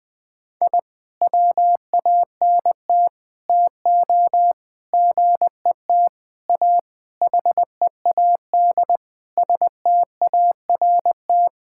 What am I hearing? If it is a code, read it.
Morse code, heard as a keyed tone, about 20 wpm, 700 Hz. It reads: I WANT TO GET A HEAD START
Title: Sentences with 9-Word Phrases - Spoken Beginning - 20wpm